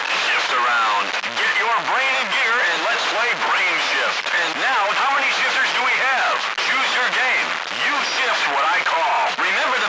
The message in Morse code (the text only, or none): none